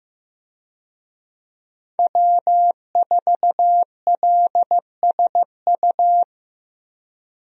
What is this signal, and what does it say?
Morse code, heard as a keyed tone, about 15 wpm, 700 Hz. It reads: W4LSU
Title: USA Call Signs - 5 Characters - Repeated 3x - 15wpm